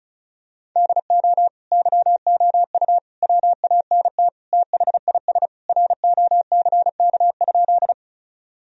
Morse code, heard as a keyed tone, about 34 wpm, 700 Hz. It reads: DO YOU WANT THIS ROCK?
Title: Sentences from Top 1000 Words - Review Part 2 - 35wpm